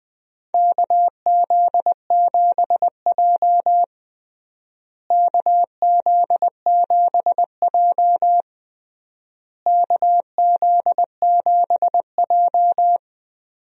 Morse code, heard as a keyed tone, about 21 wpm, 700 Hz. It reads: KZ7J KZ7J KZ7J
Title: USA Call Signs - 4 Characters - Repeated 3x - 20wpm